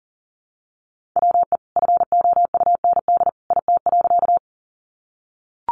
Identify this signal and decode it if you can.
Morse code, heard as a keyed tone, about 40 wpm, 700 Hz. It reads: WE FOUND IT.
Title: Sentences from Top 300 Words 40wpm